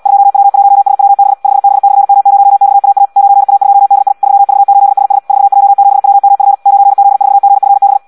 Morse code, heard as a keyed tone, about 74 wpm, 800 Hz. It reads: 2W1SWB 5B/M0XAA 9A2DI 3V8SS 4J7WMF 9Z4GFB
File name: Audio practica CW / qrq-tx-74 2